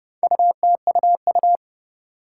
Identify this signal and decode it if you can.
Morse code, heard as a keyed tone, about 29 wpm, 700 Hz. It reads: UTUU
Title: Four Challenging Letter-Number Combinations 30wpm